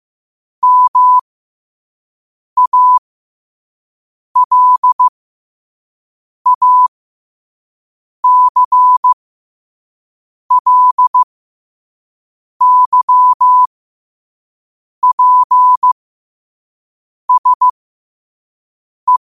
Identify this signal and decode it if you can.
Morse code, heard as a keyed tone, about 15 wpm, spaced as for 5 wpm, 1000 Hz. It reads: MALACLYPSE
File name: 5wpm